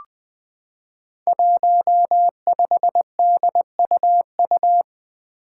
Morse code, heard as a keyed tone, about 20 wpm, 700 Hz. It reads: 15DUU
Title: Five Challenging Letter-Number Combinations 20wpm